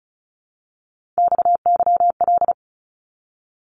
Morse code, heard as a keyed tone, about 34 wpm, 700 Hz. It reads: XYL
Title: Abbreviations Repeated 3x - 35wpm